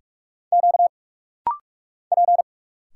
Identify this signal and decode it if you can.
Morse code, heard as a keyed tone, about 46 wpm, 700 Hz. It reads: Q P
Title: Single Letters - Rapid-Fire 45wpm